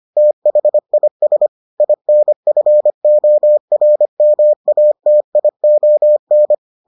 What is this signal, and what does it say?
Morse code, heard as a keyed tone, about 25 wpm, 600 Hz. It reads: THIS INFORMATION